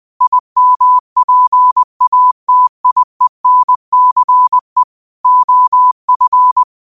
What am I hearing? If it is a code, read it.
Morse code, heard as a keyed tone, about 20 wpm, 1000 Hz. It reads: IMPATIENCE OF
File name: qotdfiles / 20wpm